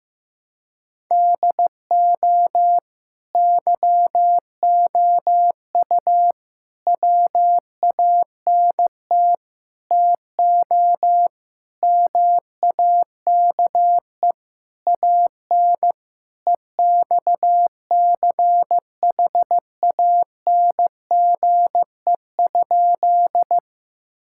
Morse code, heard as a keyed tone, about 15 wpm, 700 Hz. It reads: DO YOU WANT TO MAKE AN EXCHANGE?
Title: Sentences from Top 1000 Words - Review Part 6 - 15wpm